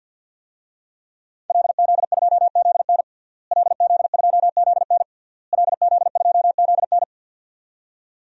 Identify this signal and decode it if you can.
Morse code, heard as a keyed tone, about 48 wpm, 700 Hz. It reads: LZ2ZN LZ2ZN LZ2ZN
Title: DXCC-100 Call Signs - 5 Characters - Repeated 3x - 50wpm